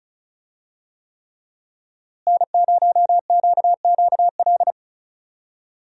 Morse code, heard as a keyed tone, about 35 wpm, 700 Hz. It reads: N0QQL
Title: USA Call Signs - 5 Characters - Repeated 3x - 35wpm